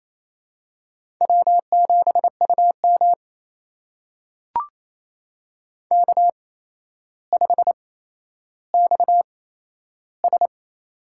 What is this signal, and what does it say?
Morse code, heard as a keyed tone, about 27 wpm, spaced as for 12 wpm, 700 Hz. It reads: W7UM K5XS
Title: USA Call Signs - 4 Characters - 8x Character Spacing - 28wpm